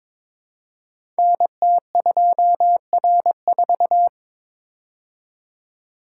Morse code, heard as a keyed tone, about 22 wpm, 700 Hz. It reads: NT2R4
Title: Code Groups - 5 Letter:Numbers - 22wpm